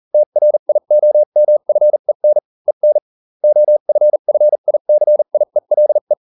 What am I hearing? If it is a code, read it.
Morse code, heard as a keyed tone, about 40 wpm, spaced as for 35 wpm, 600 Hz. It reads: TRIOMFEN EN OFFICIELE